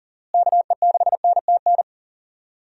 Morse code, heard as a keyed tone, about 40 wpm, 700 Hz. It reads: KE6NTN
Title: USA Call Signs - 6 Characters - Repeated 3x - 40wpm